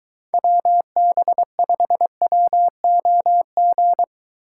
Morse code, heard as a keyed tone, about 23 wpm, 700 Hz. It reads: WB5WOG